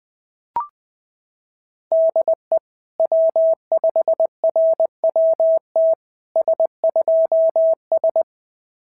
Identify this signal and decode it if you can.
Morse code, heard as a keyed tone, about 18 wpm, 650 Hz. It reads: DE W5RWT S2S
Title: SOTA QSO Practice 20wpm